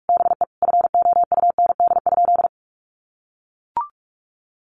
Morse code, heard as a keyed tone, about 44 wpm, 700 Hz. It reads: BE FOUND?
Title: Sentences from Top 300 Words 45wpm